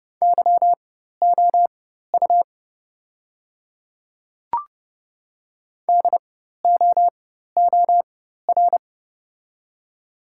Morse code, heard as a keyed tone, about 30 wpm, spaced as for 14 wpm, 690 Hz. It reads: YOU DOOR
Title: 4x Character Spacing - Top 300 Words - 30wpm